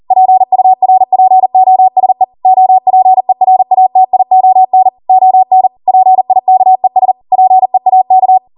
Morse code, heard as a keyed tone, about 40 wpm, 750 Hz. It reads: PURPOSE OPERATION ON PIKES PEAK